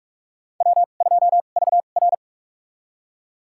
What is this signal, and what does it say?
Morse code, heard as a keyed tone, about 48 wpm, spaced as for 33 wpm, 700 Hz. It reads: W2VR
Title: USA Call Signs - 4 Characters - 2x Character Spacing - 45wpm